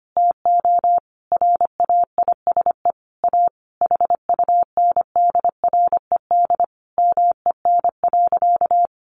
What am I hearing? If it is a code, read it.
Morse code, heard as a keyed tone, about 25 wpm, 700 Hz. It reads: TO RAISE A HUNDRED MEN.
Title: Sentences from Top 1400 Words 25wpm